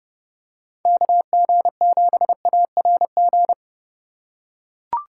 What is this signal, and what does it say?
Morse code, heard as a keyed tone, about 30 wpm, 700 Hz. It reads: KG7ARG